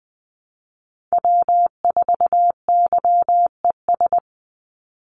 Morse code, heard as a keyed tone, about 20 wpm, 700 Hz. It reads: W4YES
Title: Call Signs Repeated 3x - 20wpm